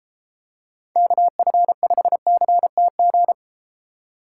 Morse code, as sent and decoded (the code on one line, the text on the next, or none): -.- ..-. ..... -.-. - --.
KF5CTG